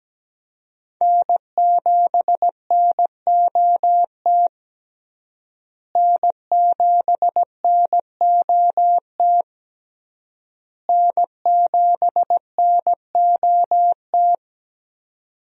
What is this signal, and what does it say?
Morse code, heard as a keyed tone, about 17 wpm, 700 Hz. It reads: N7NOT N7NOT N7NOT
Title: USA Call Signs - 5 Characters - Repeated 3x - 17wpm